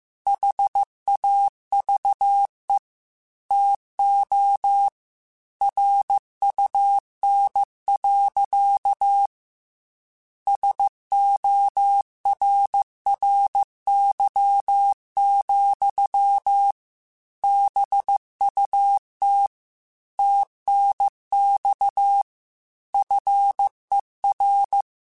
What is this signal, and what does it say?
Morse code, heard as a keyed tone, about 15 wpm, 800 Hz. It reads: HAVE TO RUN. SORRY, BUT TNX FER
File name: QSOs / QSOs-mp3s / WA7QOX 2@13